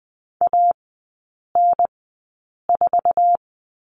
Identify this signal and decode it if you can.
Morse code, heard as a keyed tone, about 20 wpm, spaced as for 8 wpm, 700 Hz. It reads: AN4
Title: 047 Lesson 9e - Number 4 - Three Characters 20wpm